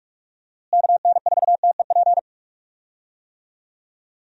Morse code, heard as a keyed tone, about 44 wpm, 700 Hz. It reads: KN4TEP
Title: USA Call Signs - 6 Characters - Repeated 3x - 45wpm